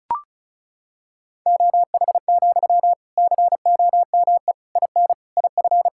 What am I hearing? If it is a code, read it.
Morse code, heard as a keyed tone, about 35 wpm, 700 Hz. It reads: OH, COME IN IF
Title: Sentences from Top 1600 Words 35wpm